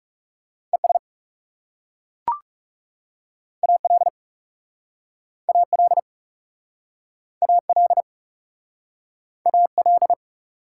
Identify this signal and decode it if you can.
Morse code, heard as a keyed tone, about 37 wpm, 700 Hz. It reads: ES AL AL AL AL
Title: Speed-Racing Prefixes and Suffixes 30wpm